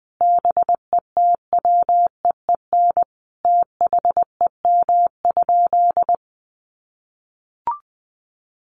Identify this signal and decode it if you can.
Morse code, heard as a keyed tone, about 20 wpm, 700 Hz. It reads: BETWEEN THEM?
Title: Sentences from Top 1300 Words 20wpm